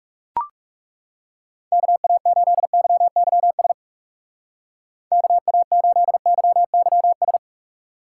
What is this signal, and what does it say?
Morse code, heard as a keyed tone, about 37 wpm, 700 Hz. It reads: KA8YYS KA8YYS
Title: Speed Racing - USA Call Signs 30wpm